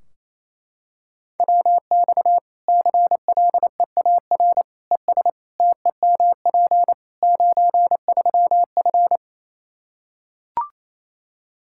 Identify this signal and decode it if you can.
Morse code, heard as a keyed tone, about 28 wpm, 700 Hz. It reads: WX CLEAR ES TEMP 93F
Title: K1USN SST QSO Practice 28wpm